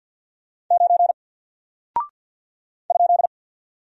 Morse code, heard as a keyed tone, about 48 wpm, 700 Hz. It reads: 9 ?
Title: Single Character (Letter:Number:Punctuation:Prosign) Alt 50wpm